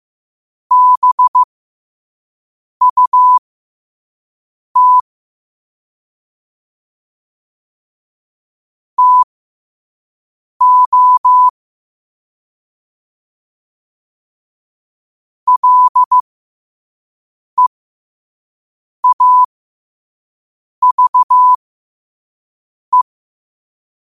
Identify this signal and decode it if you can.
Morse code, heard as a keyed tone, about 15 wpm, spaced as for 5 wpm, 1000 Hz. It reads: BUT TO LEAVE
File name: qotdfiles / 5wpm